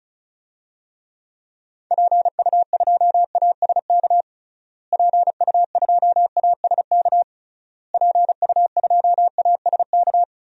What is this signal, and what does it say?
Morse code, heard as a keyed tone, about 36 wpm, 700 Hz. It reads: PU2ASK PU2ASK PU2ASK
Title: DXCC-100 Call Signs - 6 Characters - Repeated 3x - 35wpm